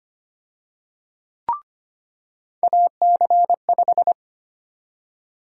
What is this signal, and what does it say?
Morse code, heard as a keyed tone, about 25 wpm, 700 Hz. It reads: AC5
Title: USA Call Sign Prefixes - 25wpm